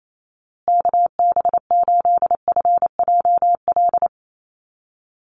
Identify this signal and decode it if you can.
Morse code, heard as a keyed tone, about 28 wpm, 700 Hz. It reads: KB8FJL
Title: USA Call Signs Repeated 3x 28wpm